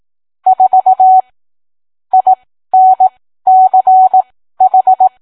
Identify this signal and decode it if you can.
Morse code, heard as a keyed tone, about 18 wpm, spaced as for 13 wpm, 750 Hz. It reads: 4 INCH